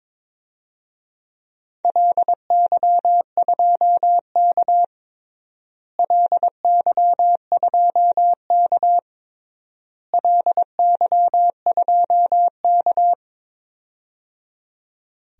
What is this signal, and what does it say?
Morse code, heard as a keyed tone, about 21 wpm, 700 Hz. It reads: LY2K LY2K LY2K
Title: DXCC-100 Call Signs - 4 Characters - Repeated 3x - 22wpm